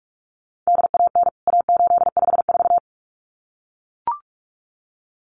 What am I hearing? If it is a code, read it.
Morse code, heard as a keyed tone, about 44 wpm, 700 Hz. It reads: DAN A854
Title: CWT Exchange 45wpm